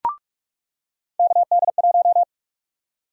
Morse code, heard as a keyed tone, about 42 wpm, 700 Hz. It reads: KD1
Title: USA Call Sign Prefixes - 45wpm